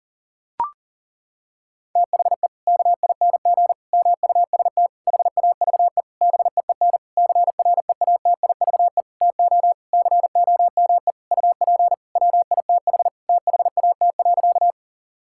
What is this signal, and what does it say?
Morse code, heard as a keyed tone, about 40 wpm, 700 Hz. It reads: THE KING MUST HAVE BEEN CREATIVE TO COME UP WITH THAT.
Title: Sentences from Top 1000 Words - Review Part 5 - 40wpm